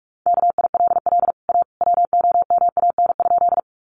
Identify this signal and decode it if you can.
Morse code, heard as a keyed tone, about 44 wpm, 700 Hz. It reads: KILL A WOMAN?
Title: Sentences from Top 1400 Words 45wpm